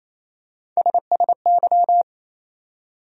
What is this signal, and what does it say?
Morse code, heard as a keyed tone, about 28 wpm, 700 Hz. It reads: SSY
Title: USA Call Sign Postfixes - 3 Letters 28wpm